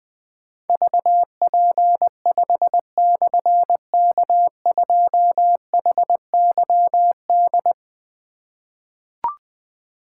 Morse code, heard as a keyed tone, about 20 wpm, 700 Hz. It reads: VP5/K2HYD
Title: Difficult Call Signs 20wpm